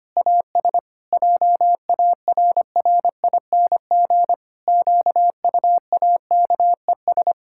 Morse code, heard as a keyed tone, about 25 wpm, 700 Hz. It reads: AS JARRING QUAKES
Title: Pangram Challenge v2 - 25wpm